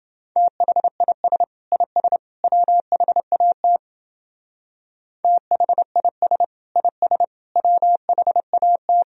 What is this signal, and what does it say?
Morse code, heard as a keyed tone, about 28 wpm, 700 Hz. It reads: THIS IS WHAT THIS IS WHAT